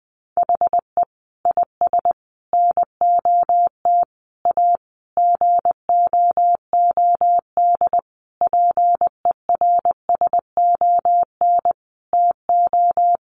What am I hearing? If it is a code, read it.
Morse code, heard as a keyed tone, about 20 wpm, 700 Hz. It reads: HE IS NOT A GOOD PERSON TO
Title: Sentences from Top 1400 Words 20wpm